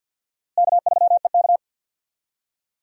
Morse code, heard as a keyed tone, about 51 wpm, 700 Hz. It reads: K3EK
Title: USA Call Signs - 4 Characters - Repeated 3x - 50wpm